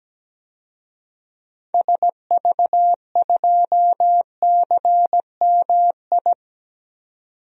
Morse code, heard as a keyed tone, about 17 wpm, 700 Hz. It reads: SV2CMI
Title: DXCC-100 Call Signs - 6 Characters - 17wpm